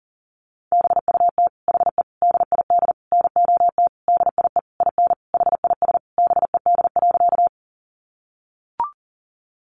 Morse code, heard as a keyed tone, about 37 wpm, 700 Hz. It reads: BUT HE DID NOT DIE IN HIS BED.
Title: Sentences from Top 1300 Words 40wpm